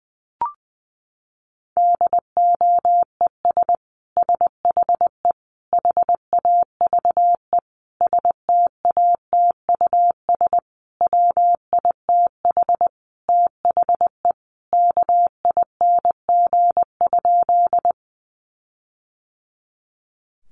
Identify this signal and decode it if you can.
Morse code, heard as a keyed tone, about 20 wpm, 700 Hz. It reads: DOES SHE HAVE STATUS WITH THE KING?
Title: Sentences from Top 1400 Words 20wpm